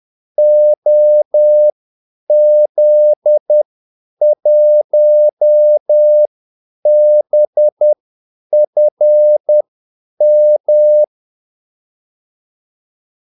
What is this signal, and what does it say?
Morse code, heard as a keyed tone, about 10 wpm, spaced as for 8 wpm, 600 Hz. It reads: OZ1BFM
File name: Callsigns-10-8wpm